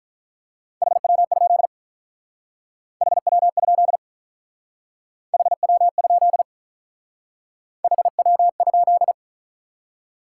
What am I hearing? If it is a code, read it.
Morse code, heard as a keyed tone, about 44 wpm, 700 Hz. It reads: HW? HW? HW? HW?